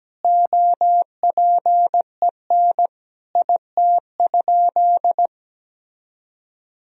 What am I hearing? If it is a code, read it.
Morse code, heard as a keyed tone, about 17 wpm, 700 Hz. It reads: OPEN IT?
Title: Sentences from Top 1000 Words - Review Part 4 - 17wpm